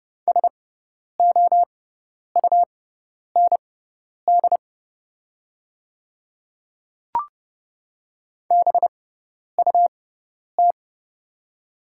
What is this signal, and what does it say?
Morse code, heard as a keyed tone, about 30 wpm, spaced as for 10 wpm, 690 Hz. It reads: SOUND BUT
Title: 6x Character Spacing - Top 200 Words - 30wpm